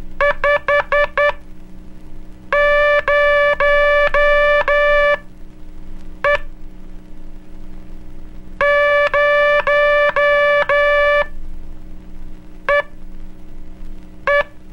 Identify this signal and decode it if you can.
Morse code, heard as a keyed tone, about 12 wpm, spaced as for 5 wpm, 580 Hz. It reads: 50E 0EE
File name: cursos / Curso CW mp3 / CW Aula - 01 - (5, zero, E)